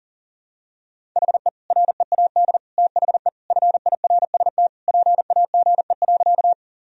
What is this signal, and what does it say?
Morse code, heard as a keyed tone, about 40 wpm, 700 Hz. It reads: HE READ THE FIRST PAGE.
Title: Sentences from Top 1000 Words - Review Part 1 - 40wpm